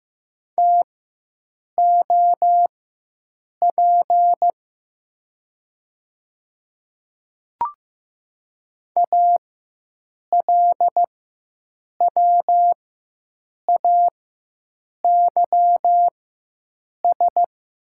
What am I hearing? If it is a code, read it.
Morse code, heard as a keyed tone, about 15 wpm, spaced as for 7 wpm, 700 Hz. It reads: TOP ALWAYS
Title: 4x Character Spacing - Top 500 Words - 15wpm